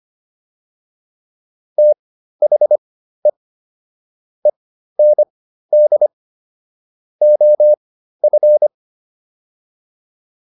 Morse code, heard as a keyed tone, about 25 wpm, spaced as for 13 wpm, 600 Hz. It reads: THE END OF